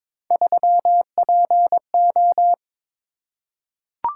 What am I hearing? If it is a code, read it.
Morse code, heard as a keyed tone, about 22 wpm, 700 Hz. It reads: 3PO
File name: practice / 22wpm / Three Letter-Number Combinations - 22wpm